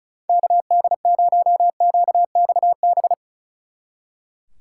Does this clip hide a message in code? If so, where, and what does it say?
Yes: Morse code from 0.3 to 3.1 s, heard as a keyed tone, about 46 wpm, 700 Hz. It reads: KD0QXB